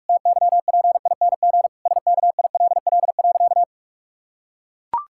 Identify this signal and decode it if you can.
Morse code, heard as a keyed tone, about 43 wpm, 700 Hz. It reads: TYPING SKILL.
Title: Pangram Challenge v2 - 45wpm